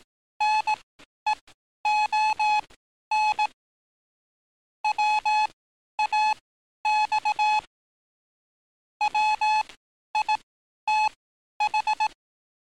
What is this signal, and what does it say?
Morse code, heard as a keyed tone, about 18 wpm, spaced as for 11 wpm, 850 Hz. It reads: NEON WAX WITH
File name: training / NAVPERS 11049to11060 / NAVPERS 11051 RA RB